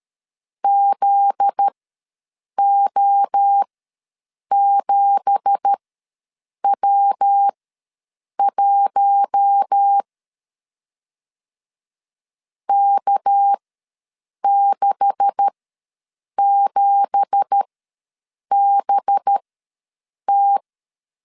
Morse code, heard as a keyed tone, about 13 wpm, spaced as for 7 wpm, 800 Hz. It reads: ZO7W1 K67BT
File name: LEZIONI / 9 2867 ws 12wpm 3431